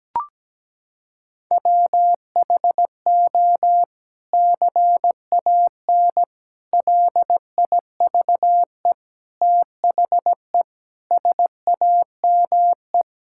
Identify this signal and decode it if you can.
Morse code, heard as a keyed tone, about 17 wpm, 700 Hz. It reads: WHO CAN LIVE THE SAME
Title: Sentences from Top 1600 Words 17wpm